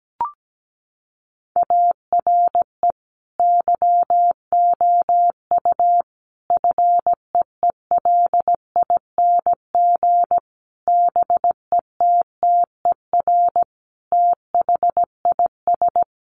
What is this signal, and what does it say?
Morse code, heard as a keyed tone, about 17 wpm, 700 Hz. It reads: ARE YOU FEELING BETTER THIS
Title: Sentences from Top 1200 Words 17wpm